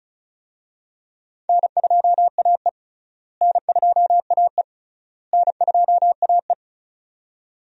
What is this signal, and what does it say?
Morse code, heard as a keyed tone, about 36 wpm, 700 Hz. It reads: N2AE N2AE N2AE
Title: USA Call Signs - 4 Characters - Repeated 3x - 35wpm